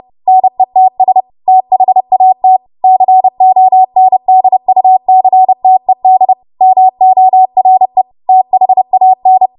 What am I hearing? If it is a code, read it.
Morse code, heard as a keyed tone, about 30 wpm, 750 Hz. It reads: NETS THAT CONDUCTED MORE THAN
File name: Archive / 30 WPM / 30WPM